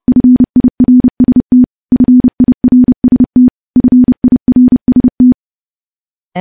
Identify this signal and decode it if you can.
Morse code, heard as a keyed tone, about 28 wpm, 250 Hz. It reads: FIRST FIRST FIRST
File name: morse / morse-training-30WPM